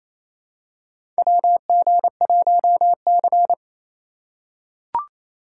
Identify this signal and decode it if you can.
Morse code, heard as a keyed tone, about 28 wpm, 700 Hz. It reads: WG1C
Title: USA Call Signs - 2x Character Spacing - 28wpm